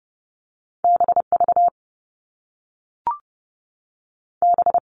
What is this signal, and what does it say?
Morse code, heard as a keyed tone, about 30 wpm, 700 Hz. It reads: BV B
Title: All Cut Numbers - 2-Digit 30wpm